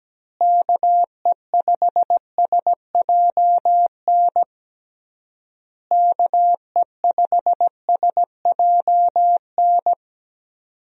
Morse code, heard as a keyed tone, about 17 wpm, 700 Hz. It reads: KE5SJN KE5SJN
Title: USA Call Signs - 6 Characters - Repeated 3x - 17wpm